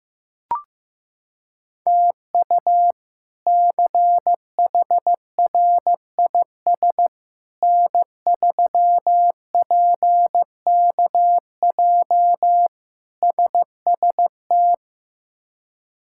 Morse code, heard as a keyed tone, about 15 wpm, 700 Hz. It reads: TU CHRIS N3PKJ SST
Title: K1USN SST QSO Practice - 15wpm